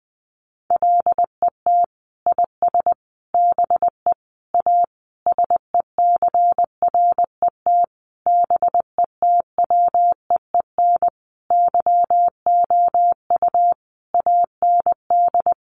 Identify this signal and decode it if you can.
Morse code, heard as a keyed tone, about 20 wpm, 700 Hz. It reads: LET IS BE A SECRET BETWEEN YOU AND
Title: Sentences from Top 1200 Words 20wpm